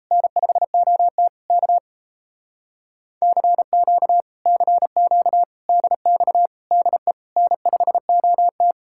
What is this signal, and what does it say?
Morse code, heard as a keyed tone, about 38 wpm, 700 Hz. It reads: N5OT K CQ CQ DX DE N5OT